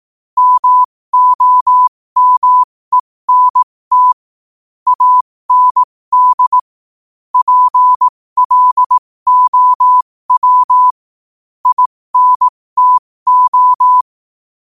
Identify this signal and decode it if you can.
Morse code, heard as a keyed tone, about 18 wpm, spaced as for 15 wpm, 1000 Hz. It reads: MOMENT AND PLOW INTO
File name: 15wpm